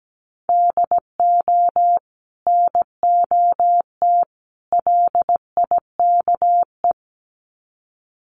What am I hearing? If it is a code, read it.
Morse code, heard as a keyed tone, about 17 wpm, 700 Hz. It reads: DO NOT LIKE
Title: Sets of 3 Words 17wpm